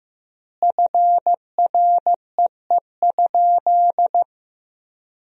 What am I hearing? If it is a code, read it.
Morse code, heard as a keyed tone, about 16 wpm, 700 Hz. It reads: FREE?
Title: 4x Word-Spacing - Sentences from Top 500 Words - 15wpm